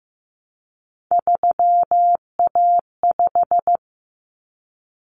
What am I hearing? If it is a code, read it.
Morse code, heard as a keyed tone, about 15 wpm, 700 Hz. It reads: 3A5
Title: Common Cut Numbers - 3-Digit 15wpm